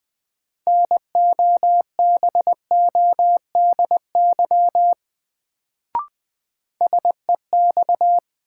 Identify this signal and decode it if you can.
Morse code, heard as a keyed tone, about 20 wpm, 700 Hz. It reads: NOBODY SEX
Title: Sentences from Top 1600 Words - New Word Drill Alt - 20wpm